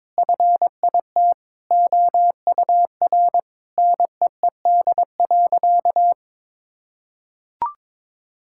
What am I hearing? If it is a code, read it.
Morse code, heard as a keyed tone, about 22 wpm, 700 Hz. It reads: FIT OUR NEED.
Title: Sentences from Top 1000 Words - Review Part 6 - 22wpm